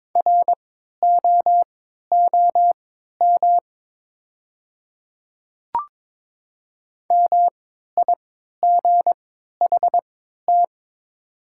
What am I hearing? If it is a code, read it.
Morse code, heard as a keyed tone, about 22 wpm, spaced as for 13 wpm, 690 Hz. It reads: ROOM MIGHT
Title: 3x Character Spacing - Top 300 Words - 22wpm